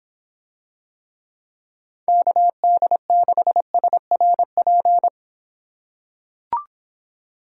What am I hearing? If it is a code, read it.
Morse code, heard as a keyed tone, about 26 wpm, 700 Hz. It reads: KD6SRP